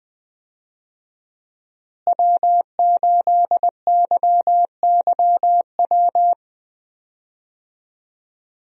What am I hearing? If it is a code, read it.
Morse code, heard as a keyed tone, about 20 wpm, 700 Hz. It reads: W8YYW
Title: USA Call Signs - 5 Characters - Repeated 3x - 20wpm